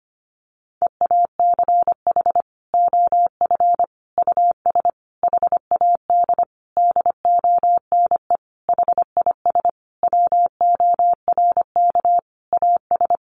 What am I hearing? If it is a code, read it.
Morse code, heard as a keyed tone, about 25 wpm, 700 Hz. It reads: EACH OF US HAD DONE HIS WORK AS
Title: Sentences from Top 500 Words 25wpm